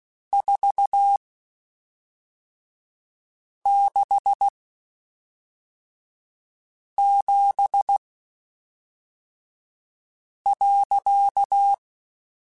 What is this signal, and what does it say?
Morse code, heard as a keyed tone, about 16 wpm, spaced as for 3 wpm, 800 Hz. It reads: 467.
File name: (spaced) / QSOs Spaced-mp3s / WA7QOX 2 @5 sp